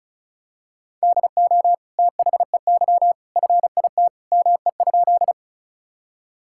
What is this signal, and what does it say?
Morse code, heard as a keyed tone, about 36 wpm, 700 Hz. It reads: DO THEY FIT ME?
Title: Sentences from Top 1000 Words - Review Part 5 - 35wpm